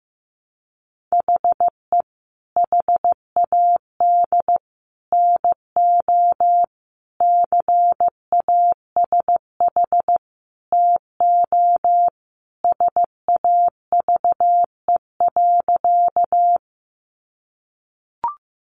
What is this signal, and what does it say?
Morse code, heard as a keyed tone, about 15 wpm, 700 Hz. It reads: HE HAD NO CASH TO SAVE.
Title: Sentences from Top 1100 Words 15wpm